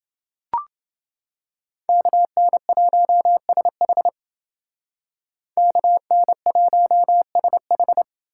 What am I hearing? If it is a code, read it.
Morse code, heard as a keyed tone, about 29 wpm, 700 Hz. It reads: KN1SH KN1SH